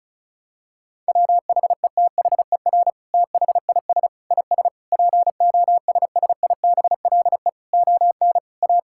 Morse code, heard as a keyed tone, about 36 wpm, 700 Hz. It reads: WHETHER THIS IS POSSIBLE ON A
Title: Sentences from Top 1000 Words - Review Part 5 - 35wpm